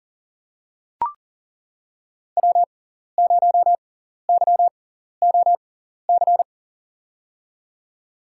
Morse code, heard as a keyed tone, about 40 wpm, spaced as for 14 wpm, 700 Hz. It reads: W0YOC
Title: USA Call Signs - 6x Character Spacing - 40wpm